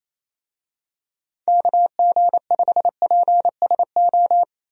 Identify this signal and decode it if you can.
Morse code, heard as a keyed tone, about 28 wpm, 700 Hz. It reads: KG5PSO
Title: USA Call Signs - 6 Characters - 28wpm